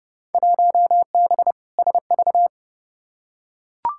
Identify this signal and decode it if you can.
Morse code, heard as a keyed tone, about 29 wpm, 700 Hz. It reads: 1B SV
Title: Field Day Exchanges - 30wpm